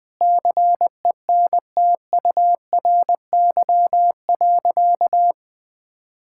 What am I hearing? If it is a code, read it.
Morse code, heard as a keyed tone, about 20 wpm, 700 Hz. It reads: CENTURY.
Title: Sentences with 6-Word Phrases - 20wpm